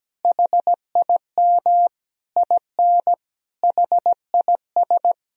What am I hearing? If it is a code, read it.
Morse code, heard as a keyed tone, about 17 wpm, 700 Hz. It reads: HIM IN HIS
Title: Sentences from Top 1000 Words - Review Part 1 - 17wpm